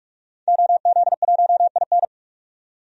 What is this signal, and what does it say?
Morse code, heard as a keyed tone, about 41 wpm, 700 Hz. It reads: OZ1IN
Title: DXCC-100 Call Signs - 5 Characters - 45wpm